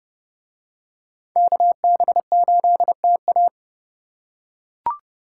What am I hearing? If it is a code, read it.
Morse code, heard as a keyed tone, about 30 wpm, 700 Hz. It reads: KB8TA